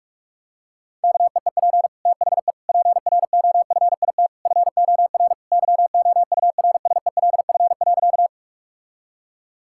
Morse code, heard as a keyed tone, about 46 wpm, 700 Hz. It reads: KEEP THE PROFIT FOR YOURSELF.
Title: Sentences from Top 1000 Words - Review Part 5 - 45wpm